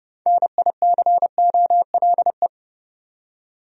Morse code, heard as a keyed tone, about 30 wpm, 700 Hz. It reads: NICOLE
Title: USA Common Names - 30wpm